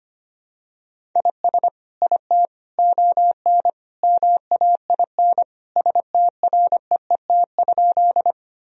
Morse code, heard as a keyed tone, about 25 wpm, 700 Hz. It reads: IS IT ON MAIN STREET?
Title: Sentences from Top 1000 Words - Review Part 2 - 25wpm